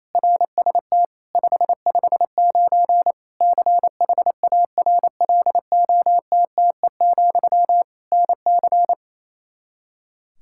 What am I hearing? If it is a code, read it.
Morse code, heard as a keyed tone, about 28 wpm, 700 Hz. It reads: RST 559 CHARLOTTE, NC
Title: Practice QSO - Exchange 1 - Beginning - 28wpm